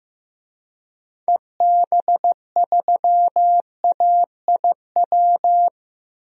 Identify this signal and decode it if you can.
Morse code, heard as a keyed tone, about 15 wpm, 700 Hz. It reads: EB3AIW
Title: DXCC-100 Call Signs - 6 Characters - Repeated 3x - 15wpm